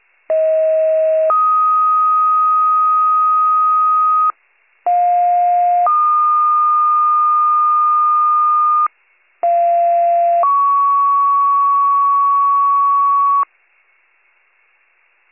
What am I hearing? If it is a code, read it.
Two-tone page, heard as a one-second tone, then a three-second tone, about 640 then 1250 Hz, about 710 then 1220 Hz, about 690 then 1120 Hz. It is heard from 0.3 to 13.4 s.